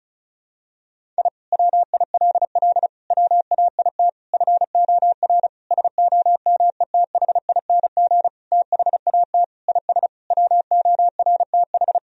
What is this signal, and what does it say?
Morse code, heard as a keyed tone, about 34 wpm, 700 Hz. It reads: I WILL WAIT FOR SOMETHING THAT IS WORTH
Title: Sentences from Top 1000 Words - Review Part 1 - 35wpm